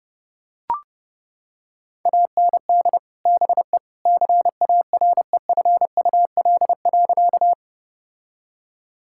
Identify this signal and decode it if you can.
Morse code, heard as a keyed tone, about 30 wpm, 700 Hz. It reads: AND BE CAREFUL.
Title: Sentences from Top 1600 Words 30wpm